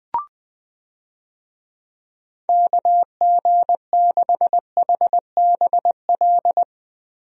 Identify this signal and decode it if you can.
Morse code, heard as a keyed tone, about 21 wpm, 700 Hz. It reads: KG6HBL
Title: USA Call Signs - 6 Characters - 20wpm